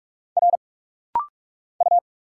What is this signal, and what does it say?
Morse code, heard as a keyed tone, about 44 wpm, 700 Hz. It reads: R U
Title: Single Letter-Number Alt - Rapid-Fire 45wpm